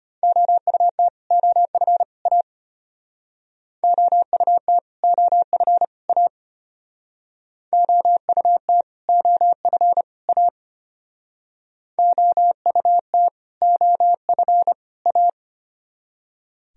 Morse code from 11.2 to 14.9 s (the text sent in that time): OUT OF